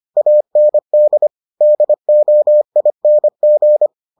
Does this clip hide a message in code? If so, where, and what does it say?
Yes: Morse code from 0.2 to 3.9 s, heard as a keyed tone, about 25 wpm, 600 Hz. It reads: AND DOING